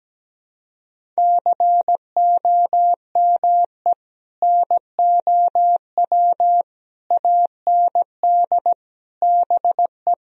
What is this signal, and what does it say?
Morse code, heard as a keyed tone, about 17 wpm, 700 Hz. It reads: COME NOW AND BE
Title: Sentences from Top 1000 Words - Review Part 4 - 17wpm